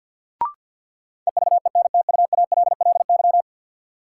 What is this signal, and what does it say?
Morse code, heard as a keyed tone, about 53 wpm, 700 Hz. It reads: EVENTUALLY
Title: Sentences from Top 1000 Words - Review Part 6 - New Word Drill - 50wpm